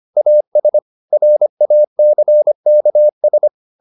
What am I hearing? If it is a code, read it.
Morse code, heard as a keyed tone, about 25 wpm, 600 Hz. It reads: AS RACKS
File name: Never 0025wpm